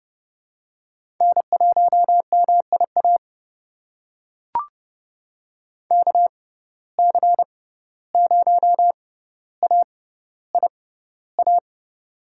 Morse code, heard as a keyed tone, about 32 wpm, spaced as for 15 wpm, 700 Hz. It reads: N1MIA KC0AIA
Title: USA Call Signs - 6x Character Spacing - 30wpm